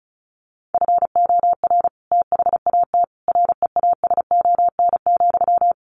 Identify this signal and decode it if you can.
Morse code, heard as a keyed tone, about 34 wpm, 700 Hz. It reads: FOR THAT REASON,
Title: Sentences from Top 1500 Words 35wpm